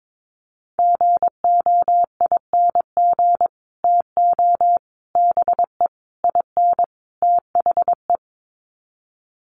Morse code, heard as a keyed tone, about 22 wpm, 700 Hz. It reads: GOING TO BE IN THE
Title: Sets of 5 Words Encore 22wpm